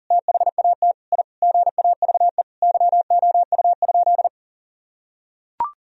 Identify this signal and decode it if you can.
Morse code, heard as a keyed tone, about 37 wpm, 700 Hz. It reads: THAT I GAVE YOU?
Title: Sentences from Top 1000 Words - Review Part 4 - 40wpm